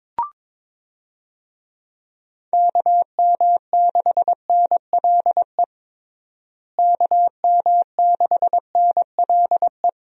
Morse code, heard as a keyed tone, about 22 wpm, 700 Hz. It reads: KM6NLE KM6NLE
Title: USA Call Signs - 6 Characters - Repeated 3x - 22wpm